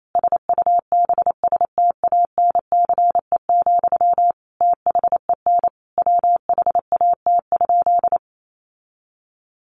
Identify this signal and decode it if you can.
Morse code, heard as a keyed tone, about 28 wpm, 700 Hz. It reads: SUBSTANCE, THEN WHAT?
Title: Sentences from Top 1500 Words 28wpm